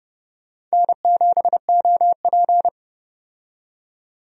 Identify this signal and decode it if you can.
Morse code, heard as a keyed tone, about 30 wpm, 700 Hz. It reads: N7OP
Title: USA Call Signs - 4 Characters - 30wpm